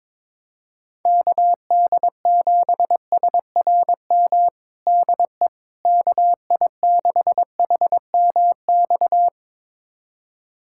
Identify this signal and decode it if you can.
Morse code, heard as a keyed tone, about 23 wpm, 700 Hz. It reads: KD7SRM DE KI6HMX
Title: USA Callsign DE Callsign 22wpm